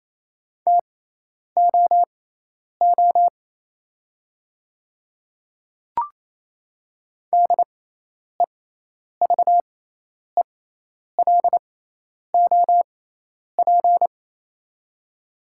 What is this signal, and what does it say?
Morse code, heard as a keyed tone, about 28 wpm, spaced as for 10 wpm, 700 Hz. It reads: TOO DEVELOP
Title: 6x Character Spacing - Top 500 Words - 28wpm